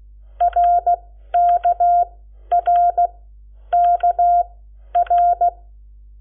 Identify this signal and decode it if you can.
Morse code, heard as a keyed tone, about 16 wpm, spaced as for 10 wpm, 700 Hz. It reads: RKRKR
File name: Lecciones / CLASE Nº7